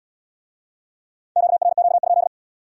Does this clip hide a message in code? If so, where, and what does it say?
Yes: Morse code from 1.4 to 2.3 s, heard as a keyed tone, about 64 wpm, 700 Hz. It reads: QRQ?